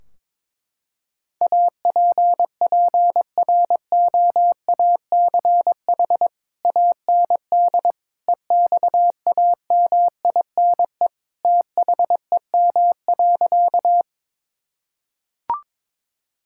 Morse code, heard as a keyed tone, about 22 wpm, 700 Hz. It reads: APPROACH AND EXAMINE THEM.
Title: Sentences from Top 1600 Words 22wpm